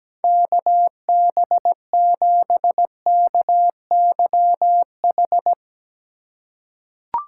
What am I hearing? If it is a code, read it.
Morse code, heard as a keyed tone, about 17 wpm, 700 Hz. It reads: KB7KYH
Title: USA Call Signs - 3x Character Spacing - 17wpm